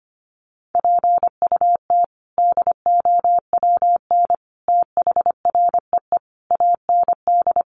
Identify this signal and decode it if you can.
Morse code, heard as a keyed tone, about 25 wpm, 700 Hz. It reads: PUT DOWN THREE AND
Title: Sentences from Top 400 Words 25wpm